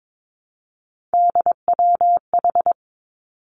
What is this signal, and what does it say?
Morse code, heard as a keyed tone, about 22 wpm, 700 Hz. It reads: DWH
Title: Three Challenging Letter-Number Combinations 22wpm